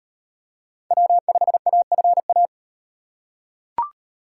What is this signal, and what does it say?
Morse code, heard as a keyed tone, about 38 wpm, 700 Hz. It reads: W5AFA